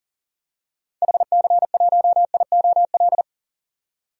Morse code, heard as a keyed tone, about 37 wpm, 700 Hz. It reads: HC1IOL
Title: DXCC-100 Call Signs - 6 Characters - 40wpm